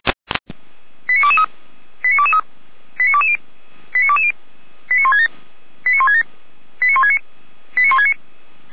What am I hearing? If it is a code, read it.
Five-tone selective call (ZVEI), heard as quick runs of five tones, tones of about 80 ms. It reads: 89223 89223 89220 89220 89167 89167 89169 89169
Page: /455/SDR/ZVEI/89123/